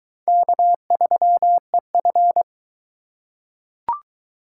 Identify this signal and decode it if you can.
Morse code, heard as a keyed tone, about 21 wpm, 700 Hz. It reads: K3EF